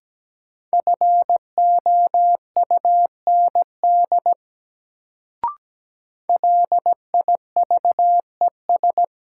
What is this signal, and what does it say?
Morse code, heard as a keyed tone, about 17 wpm, 700 Hz. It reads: FOUND LIVES
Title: Sets of 5 Words Encore - New Word Drill - 17wpm